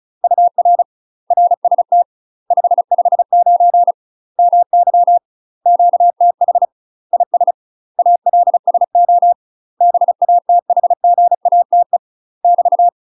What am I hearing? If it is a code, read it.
Morse code, heard as a keyed tone, about 36 wpm, 700 Hz. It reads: UR RST 559 MY QTH IS ALSO BATHGATE <BT>
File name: UK MASTER QSOs TRAINING / 35WPM / UK MASTER QSOs 0034